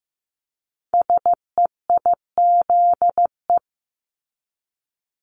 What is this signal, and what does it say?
Morse code, heard as a keyed tone, about 15 wpm, 700 Hz. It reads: SEIZE
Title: Multi-letter Phonemes Repeated 3x 15wpm